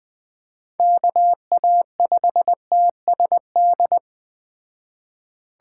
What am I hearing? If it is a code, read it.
Morse code, heard as a keyed tone, about 20 wpm, 700 Hz. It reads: KA5TSD
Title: Speed Racing - USA 2x3 Call Signs 17wpm